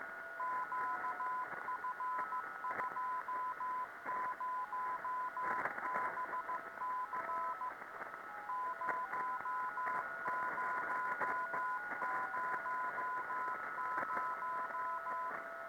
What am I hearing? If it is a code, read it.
Morse code, heard as a keyed tone, about 15 wpm, 1010 Hz, with radio noise. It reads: QF20BC 0000